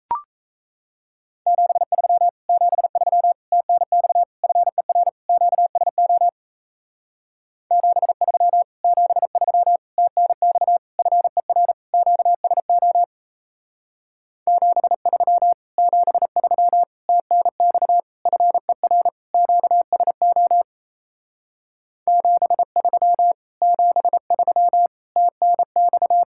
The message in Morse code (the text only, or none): 73 73 TNX FER QSO 73 73 TNX FER QSO 73 73 TNX FER QSO 73 73 TNX